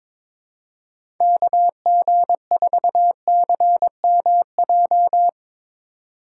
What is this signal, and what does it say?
Morse code, heard as a keyed tone, about 22 wpm, 700 Hz. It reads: KG4CMJ
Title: USA Call Signs - 2x Character Spacing - 22wpm